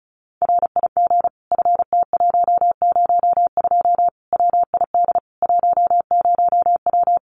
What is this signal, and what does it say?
Morse code, heard as a keyed tone, about 34 wpm, 700 Hz. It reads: RIG FT102 WID 10W
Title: Common QSO Phrases 35wpm